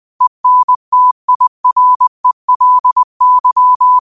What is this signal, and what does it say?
Morse code, heard as a keyed tone, about 20 wpm, 1000 Hz. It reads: ENTIRELY
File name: qotdfiles / 20wpm